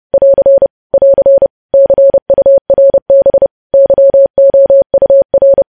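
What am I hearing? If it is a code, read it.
Morse code, heard as a keyed tone, about 30 wpm, 550 Hz. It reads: <AR> <AR> CURB YOUR
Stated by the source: News headlines sent in Morse at 30 wpm, preceded by VVV.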